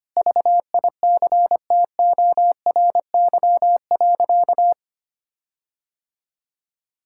Morse code, heard as a keyed tone, about 25 wpm, 700 Hz. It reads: VICTORY.
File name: practice / 25wpm / Sentences with 9-Word Phrases - 25wpm